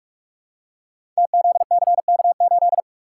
Morse code, heard as a keyed tone, about 44 wpm, 700 Hz. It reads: TZCK8
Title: Code Groups - 5 Letter:Numbers - 45wpm